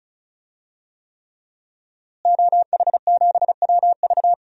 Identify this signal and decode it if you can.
Morse code, heard as a keyed tone, about 34 wpm, 700 Hz. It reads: OH7WV
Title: DXCC-100 Call Signs - 5 Characters - Repeated 3x - 35wpm